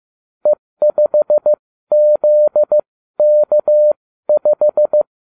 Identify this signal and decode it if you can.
Morse code, heard as a keyed tone, about 15 wpm, spaced as for 13 wpm, 600 Hz. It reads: E5ZK5